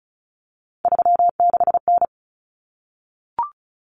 Morse code, heard as a keyed tone, about 34 wpm, 700 Hz. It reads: 36N
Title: Common Cut Numbers - 3-Digit 35wpm